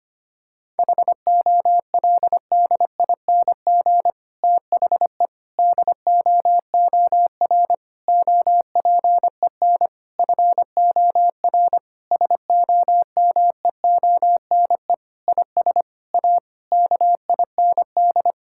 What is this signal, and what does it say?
Morse code, heard as a keyed tone, about 25 wpm, 700 Hz. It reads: HOLDING THE DOOR OPEN FOR SOMEONE IS A KIND
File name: practice / 25wpm / Difficult Sentences with Hints - 25wpm